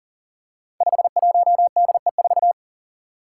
Morse code, heard as a keyed tone, about 40 wpm, 700 Hz. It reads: 51DE4
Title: Five Challenging Letter-Number Combinations 40wpm